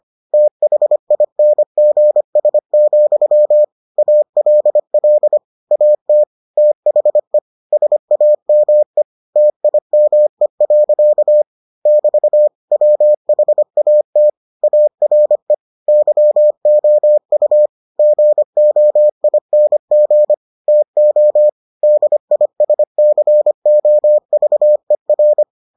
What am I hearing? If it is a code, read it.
Morse code, heard as a keyed tone, about 25 wpm, 600 Hz. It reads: THINGS, ALL AT THE SAME TIME. <BT> WHAT ARE YOU GOING TO DISCOVER